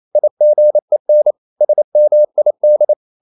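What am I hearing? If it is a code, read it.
Morse code, heard as a keyed tone, about 28 wpm, 600 Hz. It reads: IGEN SMID
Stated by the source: Danish short words sent in Morse at 28 wpm, from CW med Gnister.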